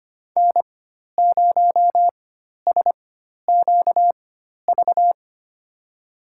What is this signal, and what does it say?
Morse code, heard as a keyed tone, about 25 wpm, spaced as for 12 wpm, 700 Hz. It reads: N0SQV
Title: USA Call Signs - 4x Character Spacing - 25wpm